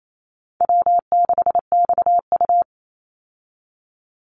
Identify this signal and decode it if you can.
Morse code, heard as a keyed tone, about 28 wpm, 700 Hz. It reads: W6XU
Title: Easy Call Signs - 28wpm